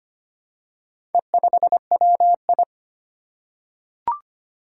Morse code, heard as a keyed tone, about 25 wpm, 700 Hz. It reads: E5WI
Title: Four Challenging Letter-Number Combinations 25wpm